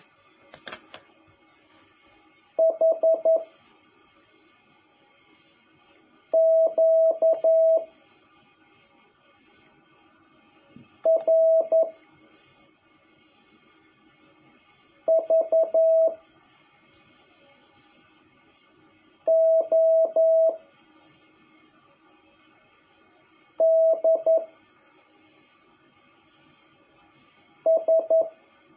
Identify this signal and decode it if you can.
Morse code, heard as a keyed tone, about 11 wpm, 650 Hz. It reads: H Q R V O D S
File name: jh8cbh / 2021cw